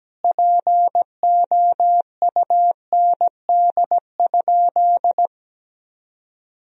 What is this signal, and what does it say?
Morse code, heard as a keyed tone, about 17 wpm, 700 Hz. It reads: POUND?
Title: Sentences from Top 1000 Words - Review Part 2 - 17wpm